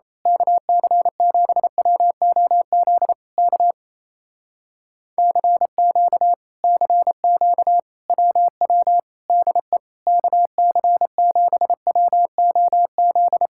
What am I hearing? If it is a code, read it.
Morse code, heard as a keyed tone, about 27 wpm, 700 Hz. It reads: KC7WOZ K CQ CQ WW DE KC7WOZ
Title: Speed-Racing Common QSO Phrases - 28wpm